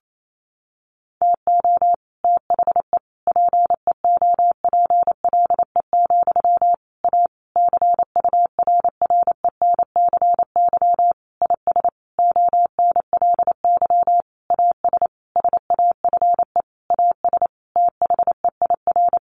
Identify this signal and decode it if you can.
Morse code, heard as a keyed tone, about 28 wpm, 700 Hz. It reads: TO THE PEOPLE, A CURRENCY IS ONLY AS SAFE AS THEIR
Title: Sentences from Top 1500 Words 28wpm